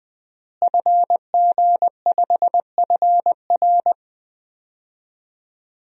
Morse code, heard as a keyed tone, about 20 wpm, 700 Hz. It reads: FG5FR
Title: DXCC-100 Call Signs - 5 Characters - Repeated 2x - 20wpm